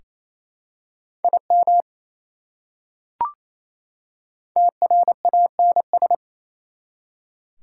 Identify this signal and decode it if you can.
Morse code, heard as a keyed tone, about 27 wpm, 700 Hz. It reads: IM TRANS
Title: Prefixes and Suffixes 28wpm